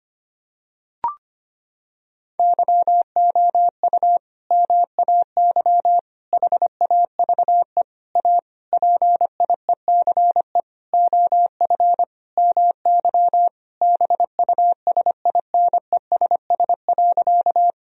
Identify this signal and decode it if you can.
Morse code, heard as a keyed tone, about 25 wpm, 700 Hz. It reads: YOU MAY HAVE A PIECE OF MY BUSINESS.
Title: Sentences from Top 1000 Words - Review Part 2 - 25wpm